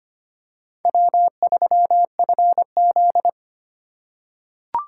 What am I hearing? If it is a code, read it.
Morse code, heard as a keyed tone, about 25 wpm, 700 Hz. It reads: W3FZ
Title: USA Call Signs - 4 Characters - 3x Character Spacing - 25wpm